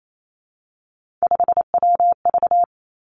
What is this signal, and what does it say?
Morse code, heard as a keyed tone, about 28 wpm, 700 Hz. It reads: 5WV
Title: Three Challenging Letter-Number Combinations 28wpm